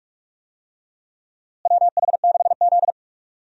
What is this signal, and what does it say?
Morse code, heard as a keyed tone, about 46 wpm, 700 Hz. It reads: WH6Z
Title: USA Call Signs - 4 Characters - Repeated 2x - 45wpm